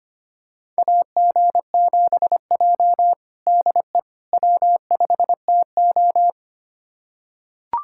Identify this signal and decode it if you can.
Morse code, heard as a keyed tone, about 25 wpm, 700 Hz. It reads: AG7J DE W5TO
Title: USA Callsign DE Callsign - 4 Characters - 25wpm